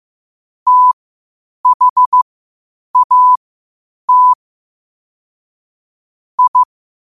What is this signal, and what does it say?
Morse code, heard as a keyed tone, about 15 wpm, spaced as for 8 wpm, 1000 Hz. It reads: THAT I
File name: qotdfiles / 8wpm